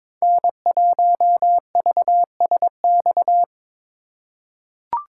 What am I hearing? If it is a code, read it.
Morse code, heard as a keyed tone, about 22 wpm, 700 Hz. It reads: N1VSX
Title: USA Call Signs - 5 Characters - 4x Character Spacing - 22wpm